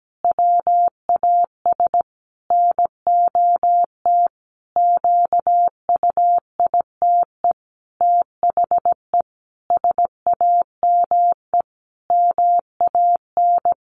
Morse code, heard as a keyed tone, about 17 wpm, 700 Hz. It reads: WAS NOT QUITE THE SAME MAN
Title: A Sound Of Thunder 17wpm